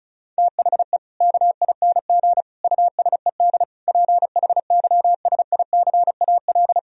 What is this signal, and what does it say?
Morse code, heard as a keyed tone, about 34 wpm, 700 Hz. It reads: THE KING USED PHYSICAL
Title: Sentences from Top 1000 Words - Review Part 4 - 35wpm